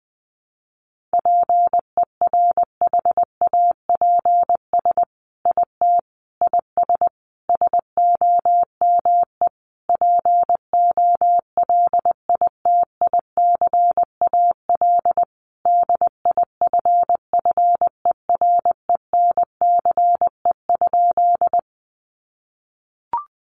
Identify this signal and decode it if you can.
Morse code, heard as a keyed tone, about 20 wpm, 700 Hz. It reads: PERHAPS IT IS SOME POLITICAL DIFFERENCE?
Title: Sentences from Top 800 Words 20wpm